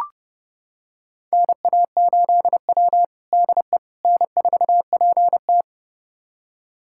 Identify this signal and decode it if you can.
Morse code, heard as a keyed tone, about 32 wpm, 700 Hz. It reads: NA8W DE N4PT
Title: USA Callsign DE Callsign - 4 Characters - 30wpm